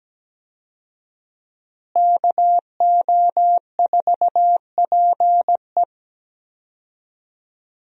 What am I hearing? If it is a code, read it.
Morse code, heard as a keyed tone, about 17 wpm, 700 Hz. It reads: KO4PE
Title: USA Call Signs - 5 Characters - Repeated 2x - 17wpm